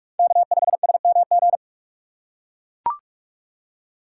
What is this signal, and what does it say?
Morse code, heard as a keyed tone, about 41 wpm, 690 Hz. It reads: K5SMG
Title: USA Call Signs - 5 Characters - 6x Character Spacing - 45wpm